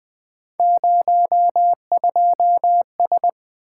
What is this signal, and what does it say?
Morse code, heard as a keyed tone, about 20 wpm, 700 Hz. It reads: 02S
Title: Three Letter-Number Combinations - 20wpm